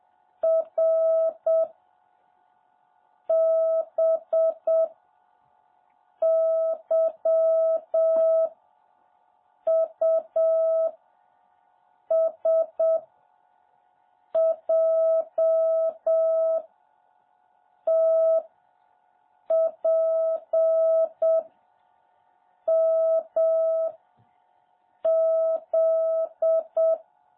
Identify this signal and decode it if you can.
Morse code, heard as a keyed tone, about 7 wpm, 650 Hz. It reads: R B Y U S J T P M Z
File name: jh8cbh / 2021cw